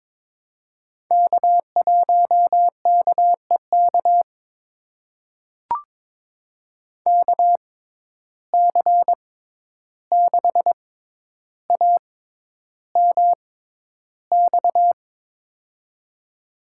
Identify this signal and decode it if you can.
Morse code, heard as a keyed tone, about 22 wpm, spaced as for 11 wpm, 700 Hz. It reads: K1KEK KC6AMX
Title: USA Call Signs - 6x Character Spacing - 22wpm